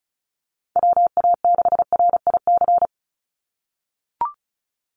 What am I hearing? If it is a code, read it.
Morse code, heard as a keyed tone, about 34 wpm, 700 Hz. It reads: WA6RIC
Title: NA SOTA Chaser Callsigns 35wpm